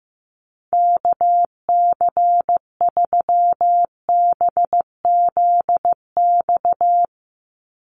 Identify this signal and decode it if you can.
Morse code, heard as a keyed tone, about 15 wpm, 700 Hz. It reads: KC3BZX
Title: USA Call Signs 15wpm